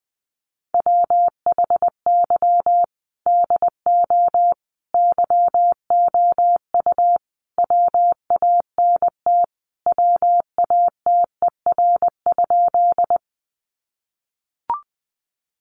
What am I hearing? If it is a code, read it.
Morse code, heard as a keyed tone, about 20 wpm, 700 Hz. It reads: WHY DO YOU WANT WATER?
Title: Sentences from Top 200 Words 20wpm